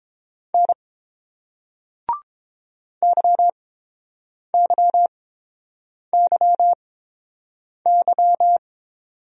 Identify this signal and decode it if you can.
Morse code, heard as a keyed tone, about 30 wpm, spaced as for 7 wpm, 700 Hz. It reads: N YYYY